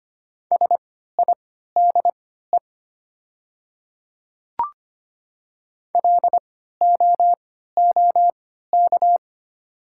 Morse code, heard as a keyed tone, about 25 wpm, spaced as for 14 wpm, 700 Hz. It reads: SIDE LOOK
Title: 3x Character Spacing - Top 100 Words - 25wpm